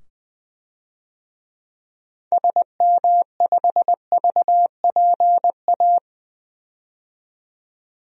Morse code, heard as a keyed tone, about 21 wpm, 700 Hz. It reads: SM5VPA
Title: DXCC-100 Call Signs - 6 Characters - 20wpm